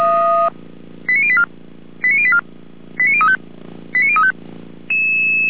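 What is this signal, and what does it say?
Five-tone selective call (ZVEI), heard as quick runs of five tones, tones of about 70 ms. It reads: 89084 89084 89035 89035
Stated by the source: ZVEI five-tone selective call received on SDR.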